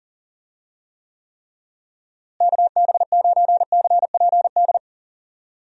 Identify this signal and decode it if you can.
Morse code, heard as a keyed tone, about 40 wpm, 700 Hz. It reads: KB9CPD
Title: USA Call Signs - 6 Characters - Repeated 3x - 40wpm